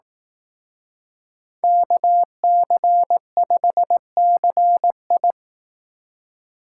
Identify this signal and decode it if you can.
Morse code, heard as a keyed tone, about 18 wpm, 700 Hz. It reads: KC5CI
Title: Speed Racing - USA 2x2 Call Signs 15wpm